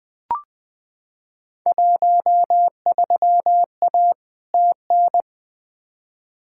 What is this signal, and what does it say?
Morse code, heard as a keyed tone, about 20 wpm, 700 Hz. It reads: 13A TN
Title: Field Day Exchanges Encore - 20wpm